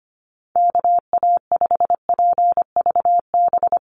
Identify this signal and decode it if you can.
Morse code, heard as a keyed tone, about 25 wpm, 700 Hz. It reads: KA5PVB
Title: NA SOTA Chaser Callsigns 25wpm